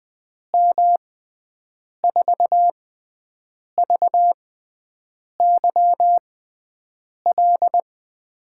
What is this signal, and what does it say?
Morse code, heard as a keyed tone, about 20 wpm, spaced as for 7 wpm, 700 Hz. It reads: M4VYL
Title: USA Call Signs - 6 Characters - 6x Character Spacing - 20wpm